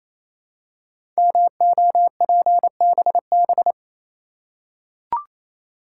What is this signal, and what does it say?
Morse code, heard as a keyed tone, about 27 wpm, 700 Hz. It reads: MOPBB
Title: Code Groups - 5 Letters - 28wpm